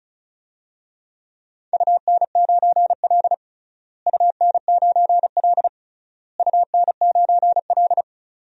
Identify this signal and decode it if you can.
Morse code, heard as a keyed tone, about 34 wpm, 700 Hz. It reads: UN9L UN9L UN9L
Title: DXCC-100 Call Signs - 4 Characters - Repeated 3x - 35wpm